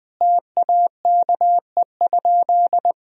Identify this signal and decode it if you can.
Morse code, heard as a keyed tone, about 21 wpm, 700 Hz. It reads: TAKE?
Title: Sentences from Top 1000 Words - Review Part 1 - 20wpm